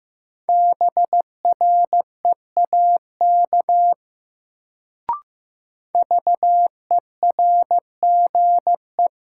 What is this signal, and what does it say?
Morse code, heard as a keyed tone, about 15 wpm, 700 Hz. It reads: BREAK VERGE
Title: Sets of 4 Words - New Word Drill - 15wpm